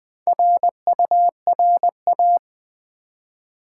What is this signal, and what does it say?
Morse code, heard as a keyed tone, about 20 wpm, 700 Hz. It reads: RURA
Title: Build A Word - 1 Letter At A Time - 5-Letter Words - 20wpm